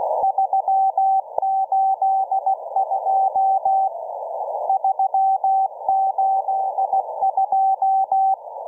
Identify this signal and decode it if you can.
Morse code, heard as a keyed tone, about 16 wpm, 770 Hz. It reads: 382 382